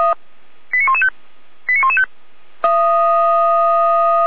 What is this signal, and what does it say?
Five-tone selective call (ZVEI), heard as quick runs of five tones, tones of about 70 ms. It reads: 89115 89115
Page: /455/SDR/ZVEI/89094/